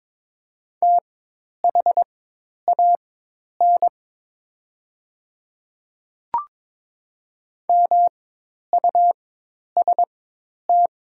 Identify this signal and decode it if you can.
Morse code, heard as a keyed tone, about 23 wpm, spaced as for 10 wpm, 700 Hz. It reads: THAN MUST
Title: 4x Character Spacing - Top 200 Words - 22wpm